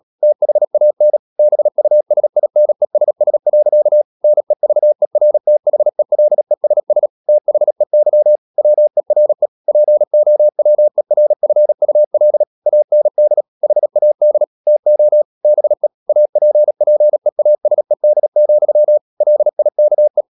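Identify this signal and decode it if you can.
Morse code, heard as a keyed tone, about 37 wpm, 600 Hz. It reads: THAN BUSINESS. NEVERTHELESS THEY WERE POWERFUL AND HAD TO BE APPEASED, LIKE
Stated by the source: Morse code practice at 35–39 wpm sending English text of Ken Follett's novel Never, from CW med Gnister.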